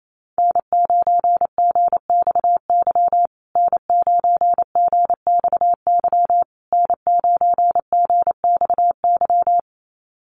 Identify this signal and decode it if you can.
Morse code, heard as a keyed tone, about 28 wpm, 700 Hz. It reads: N9GXY N9GXY N9GXY
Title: USA Call Signs Repeated 3x 28wpm